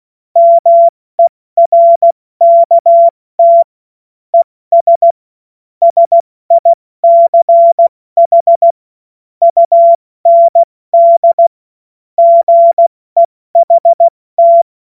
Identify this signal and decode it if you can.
Morse code, heard as a keyed tone, about 16 wpm, 680 Hz. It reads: MERKT ES SICH UND GEHT
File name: Morsekurs / Witze / Witz 24. August 2021, 70 F80